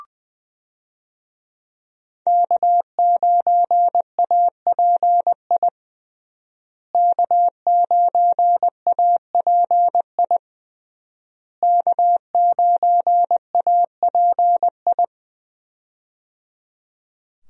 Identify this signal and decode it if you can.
Morse code, heard as a keyed tone, about 21 wpm, 700 Hz. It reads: K9API K9API K9API
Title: USA Call Signs - 5 Characters - Repeated 3x - 20wpm